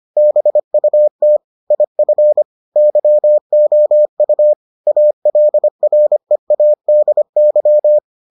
Morse code, heard as a keyed tone, about 25 wpm, 600 Hz. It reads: BUT IF YOU ALREADY